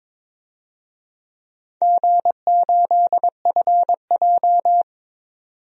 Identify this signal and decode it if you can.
Morse code, heard as a keyed tone, about 22 wpm, 700 Hz. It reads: G8FJ
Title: DXCC-100 Call Signs - 4 Characters - 22wpm